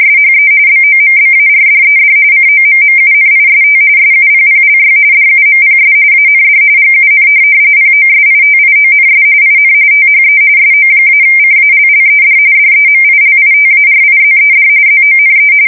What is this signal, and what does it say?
RTTY, about 45.45 baud. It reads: SHE BOMBARDED COMMUNIST SUPPLY DUMPS AND GUN POSITIONS IN THE SO-CALLED DEMILITARIZED ZONE
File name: fsk170 / USS-New-Jersey